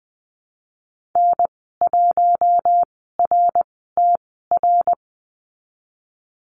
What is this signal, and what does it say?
Morse code, heard as a keyed tone, about 20 wpm, 700 Hz. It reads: N 1 R T R
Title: 055 Lesson 10g - Letter r - Call Signs 20wpm